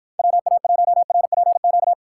Morse code, heard as a keyed tone, about 52 wpm, 700 Hz. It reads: WA1RPQ